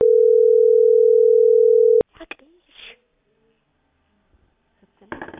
Ringback tone (North America), 440 + 480 Hz.